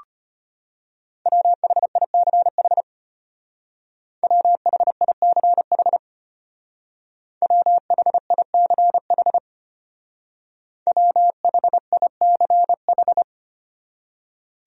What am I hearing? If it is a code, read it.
Morse code, heard as a keyed tone, about 30 wpm, 700 Hz. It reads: WHICH WHICH WHICH WHICH
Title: Speed-Racing Top 100 Words 25wpm